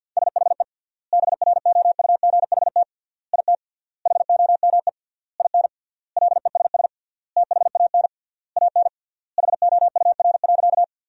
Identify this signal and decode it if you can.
Morse code, heard as a keyed tone, about 46 wpm, 700 Hz. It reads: SHE BROUGHT IT HOME IN LESS THAN AN HOUR.
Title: 3x Word-Spacing - Sentences from Top 500 Words - 50wpm